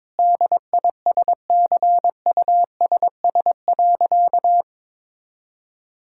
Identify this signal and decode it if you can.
Morse code, heard as a keyed tone, about 22 wpm, 700 Hz. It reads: DISCUSS.
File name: practice / 22wpm / Sentences with 2-Word Phrases - 22wpm